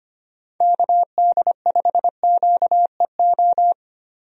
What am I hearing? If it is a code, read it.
Morse code, heard as a keyed tone, about 25 wpm, 700 Hz. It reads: KD5QEO
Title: USA Call Signs - 6 Characters - Repeated 3x - 25wpm